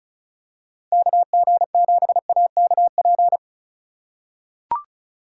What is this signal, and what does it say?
Morse code, heard as a keyed tone, about 36 wpm, 700 Hz. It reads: KG7AKP
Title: USA Call Signs - 6 Characters - 3x Character Spacing - 35wpm